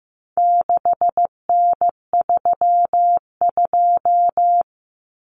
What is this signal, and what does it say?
Morse code, heard as a keyed tone, about 15 wpm, 700 Hz. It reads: 6N32
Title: Common Cut Numbers - 4-Digit 15wpm